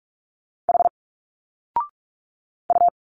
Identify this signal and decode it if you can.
Morse code, heard as a keyed tone, about 46 wpm, 700 Hz. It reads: H U
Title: Single Letters 45wpm